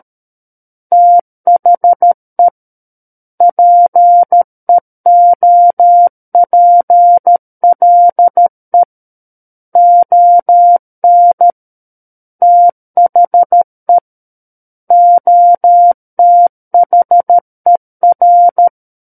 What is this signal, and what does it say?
Morse code, heard as a keyed tone, about 13 wpm, 700 Hz. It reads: THE PEOPLE ON THE OTHER
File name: Book One / 04-Chapter 4 - 13 WPM